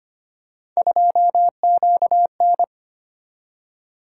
Morse code, heard as a keyed tone, about 25 wpm, 700 Hz. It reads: 2QN
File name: Three Letter-Number Combinations - 25wpm